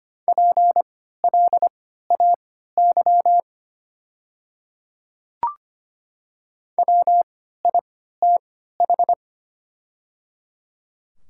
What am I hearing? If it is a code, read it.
Morse code, heard as a keyed tone, about 25 wpm, spaced as for 14 wpm, 690 Hz. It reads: PLAY WITH
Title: 3x Character Spacing - Top 200 Words - 25wpm